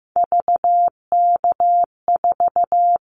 Morse code, heard as a keyed tone, about 15 wpm, 700 Hz. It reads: VK4
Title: Callsign Prefixes for Top 50 DXCC - 15wpm